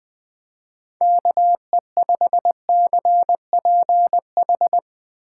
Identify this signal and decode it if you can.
Morse code, heard as a keyed tone, about 19 wpm, 700 Hz. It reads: KE5CPH
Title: Speed Racing - USA 2x3 Call Signs 17wpm